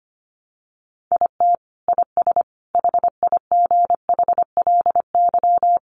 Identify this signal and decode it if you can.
Morse code, heard as a keyed tone, about 25 wpm, 690 Hz. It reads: IT IS HIGHLY
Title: Sentences from Top 1100 Words 25wpm